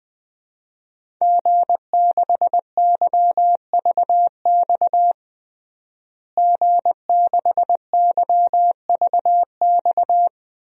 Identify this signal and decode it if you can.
Morse code, heard as a keyed tone, about 20 wpm, 700 Hz. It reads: G6YVX G6YVX
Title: DXCC-100 Call Signs - 5 Characters - Repeated 3x - 20wpm